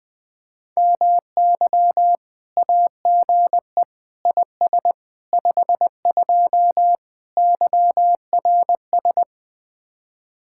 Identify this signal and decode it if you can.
Morse code, heard as a keyed tone, about 21 wpm, 700 Hz. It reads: MY AGE IS 52 YRS
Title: Speed-Racing Common QSO Phrases - 20wpm